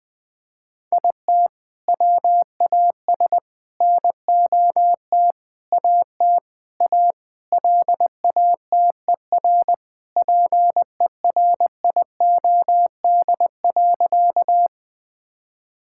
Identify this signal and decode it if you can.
Morse code, heard as a keyed tone, about 21 wpm, 700 Hz. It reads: IT WAS NOT AT A LATER PERIOD.
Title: Sentences from Top 1000 Words - Review Part 5 - 20wpm